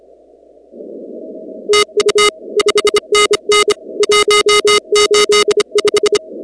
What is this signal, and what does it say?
Morse code, heard as a keyed tone, about 26 wpm, 420 Hz. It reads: TU 5NN 185
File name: contests / 2019-cw-wpx-kt5j-k5pi- -2330 0-19